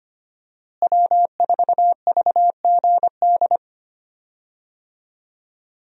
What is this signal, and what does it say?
Morse code, heard as a keyed tone, about 25 wpm, 700 Hz. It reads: W4VGD
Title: USA Call Signs - 5 Characters - Repeated 3x - 25wpm